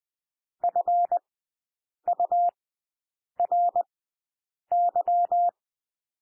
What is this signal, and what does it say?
Morse code, heard as a keyed tone, about 20 wpm, spaced as for 8 wpm, 700 Hz. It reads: FURY
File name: cwmp3 / W1 SOUND WORDS 20 wpm